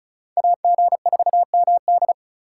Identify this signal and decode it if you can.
Morse code, heard as a keyed tone, about 46 wpm, 700 Hz. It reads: AG4MD